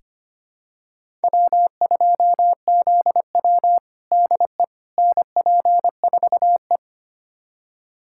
Morse code, heard as a keyed tone, about 25 wpm, 700 Hz. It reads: W2ZW DE NP4E
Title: USA Callsign DE Callsign - 4 Characters - 25wpm